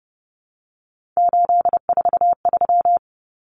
Morse code, heard as a keyed tone, about 30 wpm, 700 Hz. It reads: 843
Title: Common Cut Numbers - 3-Digit 30wpm